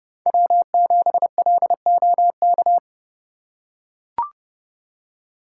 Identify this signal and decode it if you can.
Morse code, heard as a keyed tone, about 30 wpm, 700 Hz. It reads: W7LOK